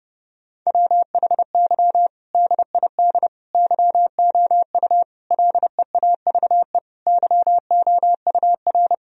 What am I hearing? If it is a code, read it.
Morse code, heard as a keyed tone, about 30 wpm, 700 Hz. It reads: WHY DID YOU LEAVE YOUR
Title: Sentences from Top 1000 Words - Review Part 3 - 30wpm